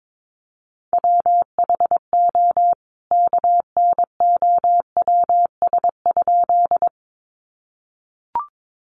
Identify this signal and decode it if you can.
Morse code, heard as a keyed tone, about 22 wpm, 700 Hz. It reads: WHO KNOWS?
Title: A Sound Of Thunder 22wpm